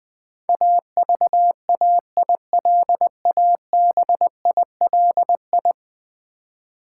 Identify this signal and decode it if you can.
Morse code, heard as a keyed tone, about 20 wpm, 700 Hz. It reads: AVAILABILI
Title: Build A Word - 5 Letters At A Time - 12-Letter Words - 20wpm